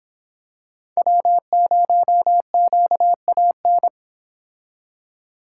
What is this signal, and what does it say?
Morse code, heard as a keyed tone, about 27 wpm, 700 Hz. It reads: W0QAN
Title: Speed Racing - USA 1x3 Call Signs 22wpm